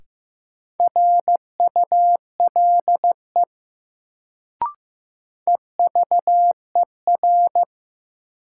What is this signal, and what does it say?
Morse code, heard as a keyed tone, about 15 wpm, 700 Hz. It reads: RULE EVER
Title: Sentences from Top 400 Words - New Word Drill - 15wpm